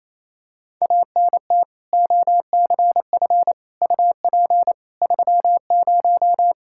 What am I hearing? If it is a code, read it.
Morse code, heard as a keyed tone, about 28 wpm, 700 Hz. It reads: ANT OCF UP 30
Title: Practice QSO - Exchange 2 - Info - 28wpm